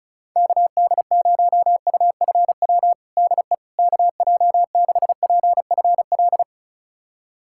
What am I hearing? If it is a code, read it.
Morse code, heard as a keyed tone, about 34 wpm, 690 Hz. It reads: KD0UFW DE KJ6PFL
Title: USA Callsign DE Callsign - 6 Characters - 35wpm